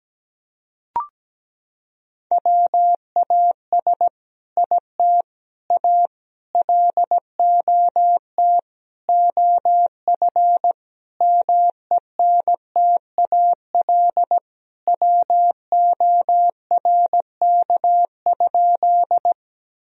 Morse code, heard as a keyed tone, about 17 wpm, 700 Hz. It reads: WAS IT A LOT OF MENTAL WORK?
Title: Sentences from Top 1000 Words - Review Part 4 - 17wpm